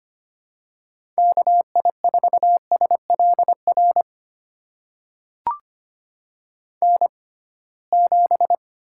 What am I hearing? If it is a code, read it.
Morse code, heard as a keyed tone, about 25 wpm, spaced as for 19 wpm, 700 Hz. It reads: KI4SLR N7
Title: USA Call Signs - 6x Character Spacing - 25wpm